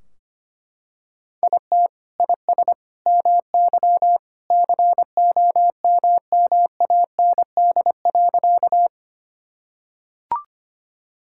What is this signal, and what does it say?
Morse code, heard as a keyed tone, about 25 wpm, 700 Hz. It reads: IT IS MY COMMAND.
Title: Sentences from Top 1600 Words 25wpm